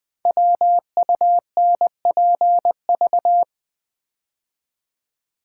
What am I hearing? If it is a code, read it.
Morse code, heard as a keyed tone, about 20 wpm, 700 Hz. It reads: WUNPV
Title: Code Groups - 5 Letters - 20wpm